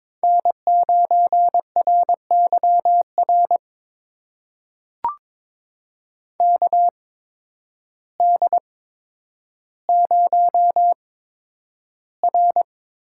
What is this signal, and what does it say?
Morse code, heard as a keyed tone, about 22 wpm, spaced as for 10 wpm, 700 Hz. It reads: N9RYR KD0R
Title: USA Call Signs - 8x Character Spacing - 22wpm